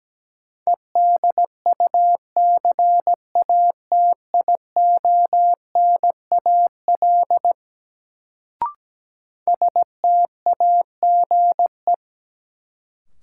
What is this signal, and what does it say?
Morse code, heard as a keyed tone, about 17 wpm, 700 Hz. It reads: EDUCATIONAL STAGE
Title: Sentences from Top 1200 Words - New Word Drill - 17wpm